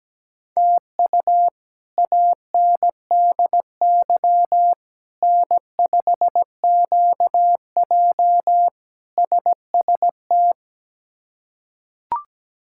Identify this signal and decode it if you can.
Morse code, heard as a keyed tone, about 17 wpm, 700 Hz. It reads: TU ANDY N5QJ SST
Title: K1USN SST QSO Practice - 17wpm